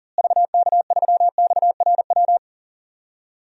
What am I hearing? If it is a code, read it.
Morse code, heard as a keyed tone, about 40 wpm, 700 Hz. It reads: VK3XRW
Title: DXCC-100 Call Signs - 6 Characters - Repeated 2x - 40wpm